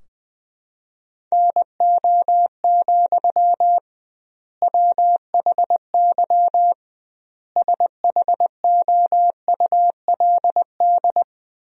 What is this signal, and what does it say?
Morse code, heard as a keyed tone, about 21 wpm, 700 Hz. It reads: NO, WHY SHOULD
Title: 2x Word-Spacing - Sentences from Top 300 Words - 20wpm